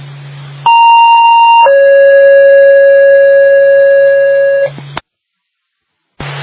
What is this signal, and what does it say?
Two-tone page, heard as a one-second tone, then a three-second tone, about 930 then 550 Hz. It is heard from 0.7 to 4.7 s.